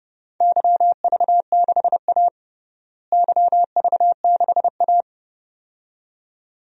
Morse code, heard as a keyed tone, about 30 wpm, 700 Hz. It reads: YV6A YV6A
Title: DXCC-100 Call Signs - 4 Characters - Repeated 2x - 30wpm